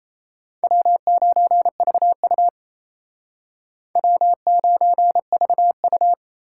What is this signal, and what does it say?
Morse code, heard as a keyed tone, about 35 wpm, 700 Hz. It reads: W9VU W9VU
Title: Speed Racing - USA 1x2 Call Signs 28wpm